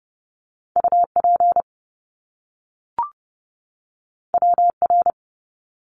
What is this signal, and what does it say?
Morse code, heard as a keyed tone, about 30 wpm, 700 Hz. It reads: UP WR
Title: Top Two Letter Combinations in Words 30wpm